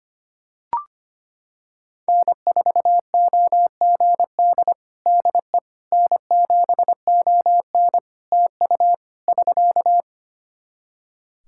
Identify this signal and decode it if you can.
Morse code, heard as a keyed tone, about 25 wpm, 700 Hz. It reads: N4OGD DE N7ON TU <SK>
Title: Practice QSO - Exchange 4 - Ending - 25wpm